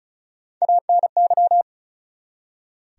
Morse code, heard as a keyed tone, about 51 wpm, spaced as for 43 wpm, 700 Hz. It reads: ANY
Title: Speed-Racing Top 500 Words 35wpm